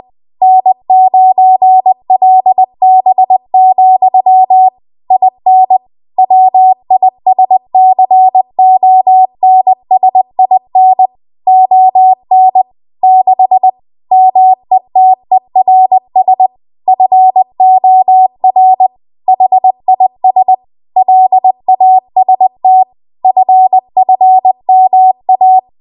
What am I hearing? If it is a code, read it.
Morse code, heard as a keyed tone, about 20 wpm, 750 Hz. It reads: N9LB, IN WISCONSIN ON 6 METERS FOR HIS LAST FFMA
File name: Morse / 20WPM